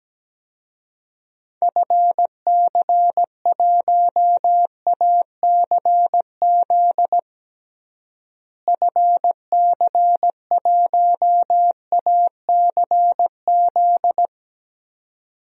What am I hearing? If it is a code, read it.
Morse code, heard as a keyed tone, about 17 wpm, 700 Hz. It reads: FC1ACZ FC1ACZ
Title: DXCC-100 Call Signs - 6 Characters - Repeated 2x - 17wpm